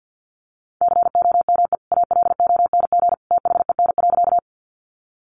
Morse code, heard as a keyed tone, about 50 wpm, 700 Hz. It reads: COME ALONG THEN.
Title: Sentences from Top 700 Words 50wpm